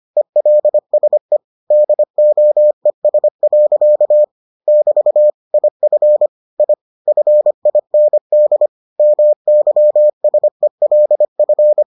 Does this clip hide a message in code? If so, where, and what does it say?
Yes: Morse code from 0.2 to 11.8 s, heard as a keyed tone, about 25 wpm, 600 Hz. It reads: ELSE DOES. <BT> IF I FIND MYSELF